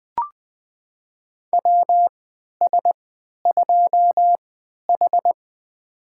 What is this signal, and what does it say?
Morse code, heard as a keyed tone, about 20 wpm, spaced as for 11 wpm, 700 Hz. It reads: WS2H
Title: USA Call Signs - 4 Characters - 3x Character Spacing - 20wpm